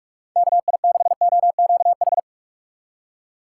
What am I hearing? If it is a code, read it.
Morse code, heard as a keyed tone, about 46 wpm, 700 Hz. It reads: KI6OQH
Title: USA Call Signs - 6 Characters - 4x Character Spacing - 45wpm